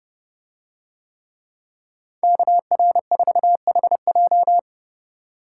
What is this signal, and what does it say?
Morse code, heard as a keyed tone, about 30 wpm, 700 Hz. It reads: KR4HJ